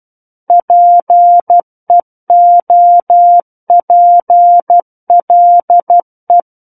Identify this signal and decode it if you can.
Morse code, heard as a keyed tone, about 12 wpm, 700 Hz. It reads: PEOPLE
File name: Book One / 03-Chapter 3 - 12 WPM